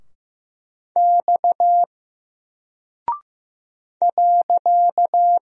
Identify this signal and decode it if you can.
Morse code, heard as a keyed tone, about 15 wpm, 700 Hz. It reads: X .
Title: Single Character (Letter:Number:Punctuation:Prosign) Alt 15wpm